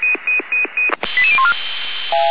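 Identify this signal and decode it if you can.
Five-tone selective call (ZVEI), heard as quick runs of five tones, tones of about 70 ms. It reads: 88014
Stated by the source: ZVEI five-tone selective call 88099 received on SDR.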